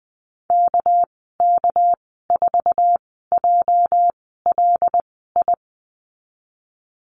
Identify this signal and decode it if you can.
Morse code, heard as a keyed tone, about 20 wpm, spaced as for 15 wpm, 700 Hz. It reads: KK4JLI
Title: 213 Lesson 34g - Number 8 - Call Signs 20wpm